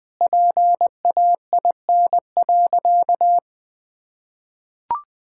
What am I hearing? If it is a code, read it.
Morse code, heard as a keyed tone, about 20 wpm, 700 Hz. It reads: PAIN.
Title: Sentences from Top 1600 Words 20wpm